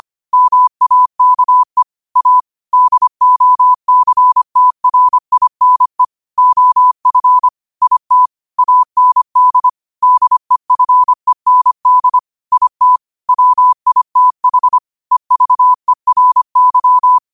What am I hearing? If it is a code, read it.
Morse code, heard as a keyed tone, about 25 wpm, 1000 Hz. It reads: MAKE A DOCTRINE OF IT AND DEFEND IT WITH EVERY
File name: qotdfiles / 25wpm